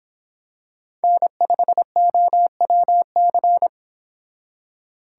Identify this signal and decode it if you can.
Morse code, heard as a keyed tone, about 26 wpm, 700 Hz. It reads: N5OWC